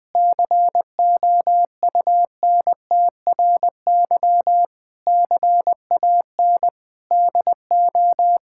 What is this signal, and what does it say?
Morse code, heard as a keyed tone, about 20 wpm, 700 Hz. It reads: COUNTRY CAN DO
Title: Sets of 8 Words - 20wpm